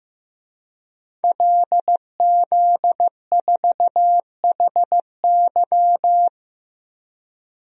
Morse code, heard as a keyed tone, about 15 wpm, 700 Hz. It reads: LZ4HY
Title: DXCC-100 Call Signs - 5 Characters - Repeated 3x - 15wpm